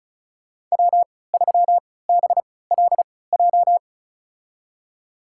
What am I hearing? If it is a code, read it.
Morse code, heard as a keyed tone, about 34 wpm, spaced as for 20 wpm, 700 Hz. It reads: W3BLJ
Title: USA Call Signs - 3x Character Spacing - 35wpm